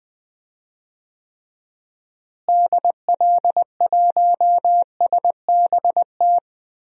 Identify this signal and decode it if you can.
Morse code, heard as a keyed tone, about 20 wpm, 700 Hz. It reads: DL1SBT
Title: DXCC-100 Call Signs - 6 Characters - Repeated 3x - 20wpm